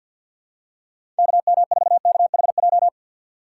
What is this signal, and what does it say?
Morse code, heard as a keyed tone, about 53 wpm, 700 Hz. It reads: KM4KHJ
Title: USA Call Signs - 4x Character Spacing - 50wpm